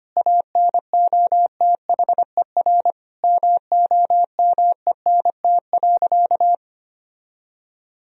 Morse code, heard as a keyed tone, about 25 wpm, 700 Hz. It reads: ANOTHER MOMENT.
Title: Sentences from Top 1000 Words - Review Part 1 - 25wpm